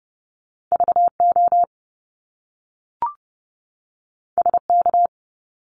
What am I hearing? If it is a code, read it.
Morse code, heard as a keyed tone, about 30 wpm, 700 Hz. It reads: VO SK
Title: Top Two Letter Combinations in Words 30wpm